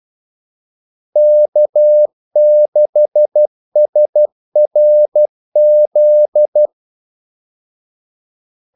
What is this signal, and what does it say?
Morse code, heard as a keyed tone, about 12 wpm, 600 Hz. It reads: K6SRZ
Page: 2 ekstra ordmellemrum mellem kaldesignalerne